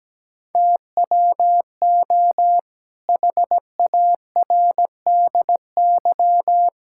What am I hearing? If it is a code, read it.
Morse code, heard as a keyed tone, about 17 wpm, 700 Hz. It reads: TWO HARDY
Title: Pangram Challenge v2 - 17wpm